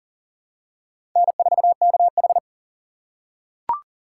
Morse code, heard as a keyed tone, about 40 wpm, 700 Hz. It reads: N4KH
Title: K1USN SST QSO Practice - 40wpm